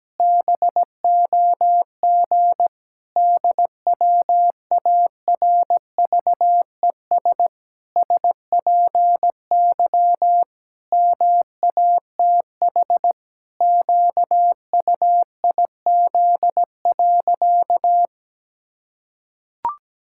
Morse code, heard as a keyed tone, about 17 wpm, 700 Hz. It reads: BOG DWARVES SPY MATH QUIZ.
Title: Pangram Challenge v2 - 17wpm